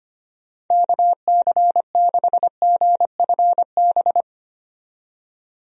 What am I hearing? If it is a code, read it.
Morse code, heard as a keyed tone, about 25 wpm, 700 Hz. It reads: KC6GFB
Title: USA Call Signs - 6 Characters - 25wpm